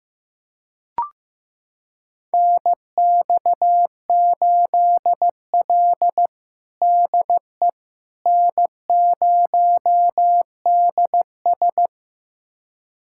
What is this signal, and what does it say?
Morse code, heard as a keyed tone, about 15 wpm, 700 Hz. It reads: NX8L DE N0DS
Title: USA Callsign DE Callsign - 4 Characters - 15wpm